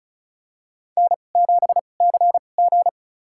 Morse code, heard as a keyed tone, about 36 wpm, spaced as for 25 wpm, 700 Hz. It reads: N7CG
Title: USA Call Signs - 4 Characters - 2x Character Spacing - 35wpm